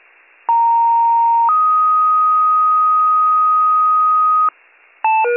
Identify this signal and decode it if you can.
Two-tone page, heard as a one-second tone, then a three-second tone, about 930 then 1250 Hz. It is heard from 0.5 to 4.5 s.